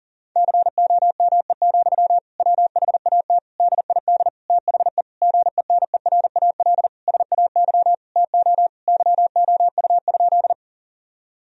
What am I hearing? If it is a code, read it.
Morse code, heard as a keyed tone, about 40 wpm, 700 Hz. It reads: COME, WHAT DID THE GENERAL SAY TO YOU?
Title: Sentences from Top 1000 Words - Review Part 3 - 40wpm